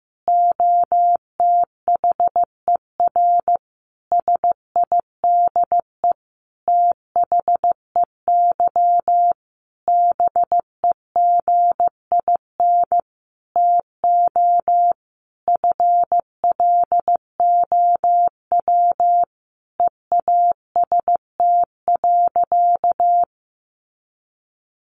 Morse code, heard as a keyed tone, about 15 wpm, 700 Hz. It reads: OTHER SIDE THEY BEGIN TO FLOW EAST.
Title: Sentences from Top 1200 Words 15wpm